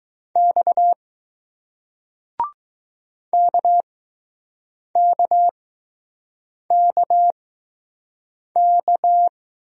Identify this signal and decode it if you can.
Morse code, heard as a keyed tone, about 20 wpm, spaced as for 6 wpm, 700 Hz. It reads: X KKKK